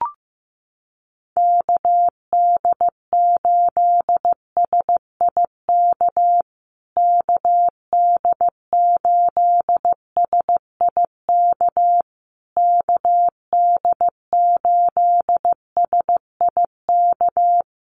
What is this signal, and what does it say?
Morse code, heard as a keyed tone, about 15 wpm, 700 Hz. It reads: KD8SIK KD8SIK KD8SIK
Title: USA Call Signs Repeated 3x 15wpm